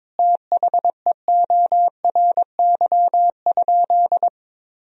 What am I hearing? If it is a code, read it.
Morse code, heard as a keyed tone, about 22 wpm, 700 Hz. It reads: THEORY?
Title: Sentences from Top 1000 Words - Review Part 3 - 22wpm